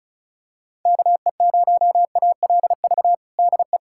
Morse code, heard as a keyed tone, about 34 wpm, 700 Hz. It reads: KE0ALV DE
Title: USA Callsign DE Callsign - 6 Characters - 35wpm